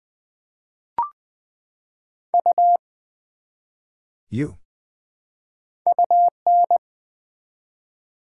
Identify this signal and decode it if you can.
Morse code, heard as a keyed tone, about 20 wpm, 700 Hz. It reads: U UN
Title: Build A Word - 1 Letter At A Time - 3 to 5-Letter Words - 20wpm